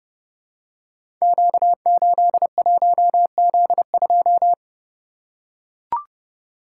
Morse code, heard as a keyed tone, about 30 wpm, 700 Hz. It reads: Q81Z2
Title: Code Groups - 5 Letter:Numbers - 30wpm